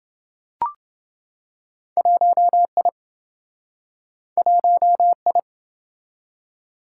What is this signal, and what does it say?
Morse code, heard as a keyed tone, about 27 wpm, 700 Hz. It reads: 1I 1I